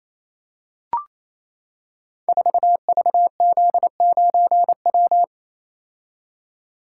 Morse code, heard as a keyed tone, about 27 wpm, 700 Hz. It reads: 4VZ9W
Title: Code Groups - 5 Letter:Numbers - 28wpm